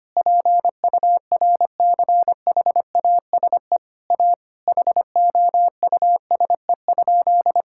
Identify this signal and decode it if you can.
Morse code, heard as a keyed tone, about 25 wpm, 700 Hz. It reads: PURCHASE A HOUSE?
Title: Sentences from Top 1000 Words - Review Part 5 - 25wpm